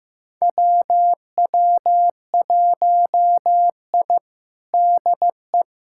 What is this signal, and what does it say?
Morse code, heard as a keyed tone, about 15 wpm, 700 Hz. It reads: WW1I DE
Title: USA Callsign DE Callsign - 4 Characters - 15wpm